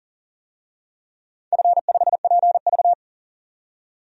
Morse code, heard as a keyed tone, about 40 wpm, 700 Hz. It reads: F5PV
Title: DXCC-100 Call Signs - 4 Characters - 40wpm